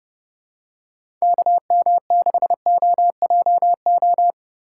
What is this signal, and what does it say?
Morse code, heard as a keyed tone, about 30 wpm, 700 Hz. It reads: KM6OJO